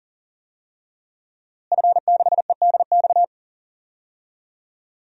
Morse code, heard as a keyed tone, about 37 wpm, 700 Hz. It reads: F6EDX
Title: DXCC-100 Call Signs - 5 Characters - Repeated 3x - 40wpm